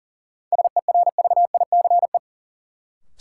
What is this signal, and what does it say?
Morse code, heard as a keyed tone, about 40 wpm, 700 Hz. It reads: SERVICE
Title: Sentences from Top 700 Words - New Word Drill - 40wpm